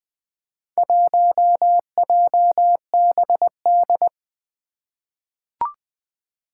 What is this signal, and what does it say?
Morse code, heard as a keyed tone, about 20 wpm, 700 Hz. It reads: 1JBD
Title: Four Challenging Letter-Number Combinations 20wpm